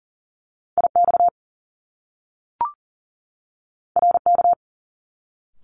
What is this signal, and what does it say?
Morse code, heard as a keyed tone, about 42 wpm, 700 Hz. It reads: IX RK
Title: Top Two Letter Combinations in Words 40wpm